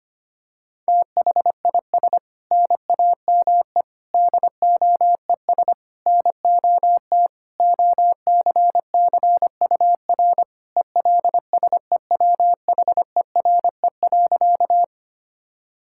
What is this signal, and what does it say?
Morse code, heard as a keyed tone, about 25 wpm, 700 Hz. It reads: THIS NAME DOES NOT OCCUR ELSEWHERE.
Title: Sentences from Top 1600 Words 25wpm